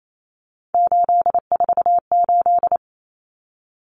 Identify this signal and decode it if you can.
Morse code, heard as a keyed tone, about 28 wpm, 700 Hz. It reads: 848
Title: Common Cut Numbers - 3-Digit 28wpm